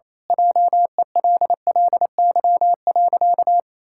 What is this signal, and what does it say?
Morse code, heard as a keyed tone, about 28 wpm, 700 Hz. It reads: JELLY.
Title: Pangram Challenge v2 - 28wpm